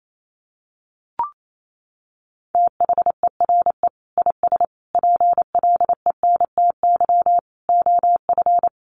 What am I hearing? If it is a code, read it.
Morse code, heard as a keyed tone, about 28 wpm, 700 Hz. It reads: THERE IS PLENTY OF
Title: Sentences from Top 1300 Words 28wpm